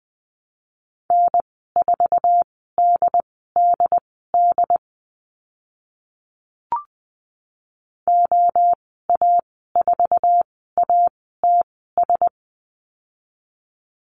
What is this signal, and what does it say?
Morse code, heard as a keyed tone, about 20 wpm, spaced as for 15 wpm, 700 Hz. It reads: N4DDD OA4ATS
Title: 076 Lesson 13g - Letter L - Call Signs 20wpm